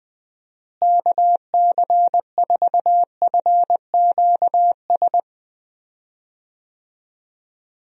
Morse code, heard as a keyed tone, about 20 wpm, 700 Hz. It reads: KC4FQS
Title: USA Call Signs - 6 Characters - Repeated 3x - 20wpm